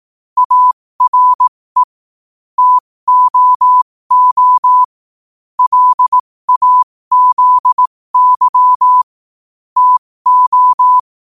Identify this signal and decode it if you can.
Morse code, heard as a keyed tone, about 18 wpm, spaced as for 15 wpm, 1000 Hz. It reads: ARE TOO LAZY TO
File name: qotdfiles / 15wpm